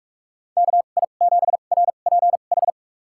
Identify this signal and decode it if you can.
Morse code, heard as a keyed tone, about 44 wpm, spaced as for 32 wpm, 700 Hz. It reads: KI7RPH
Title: USA Call Signs - 6 Characters - 2x Character Spacing - 45wpm